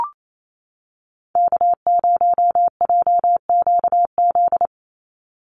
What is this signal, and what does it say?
Morse code, heard as a keyed tone, about 28 wpm, 700 Hz. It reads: K0JQZ
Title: NA SOTA Activator Callsigns 28wpm